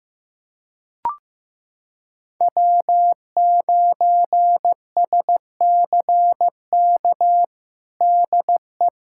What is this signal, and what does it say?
Morse code, heard as a keyed tone, about 15 wpm, 700 Hz. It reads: W9SCK DE
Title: USA Callsign DE Callsign - 5 Characters - 15wpm